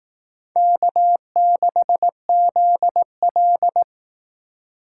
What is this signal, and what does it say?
Morse code, heard as a keyed tone, about 19 wpm, 700 Hz. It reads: K6ZL
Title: Speed Racing - USA 1x2 Call Signs 15wpm